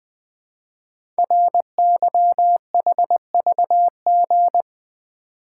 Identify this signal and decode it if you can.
Morse code, heard as a keyed tone, about 20 wpm, 700 Hz. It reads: RYHVG
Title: Code Groups - 5 Letters - 20wpm